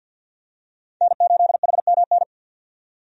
Morse code, heard as a keyed tone, about 49 wpm, 700 Hz. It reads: N8HMN
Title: USA Call Signs - 5 Characters - 50wpm